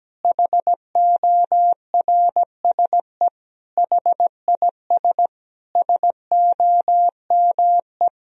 Morse code, heard as a keyed tone, about 17 wpm, 700 Hz. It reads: HORSE HIS SOME
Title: Sentences from Top 1000 Words - Review Part 1 - 17wpm